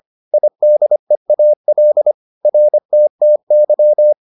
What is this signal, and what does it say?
Morse code, heard as a keyed tone, about 25 wpm, 600 Hz. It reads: IDEAL RTTY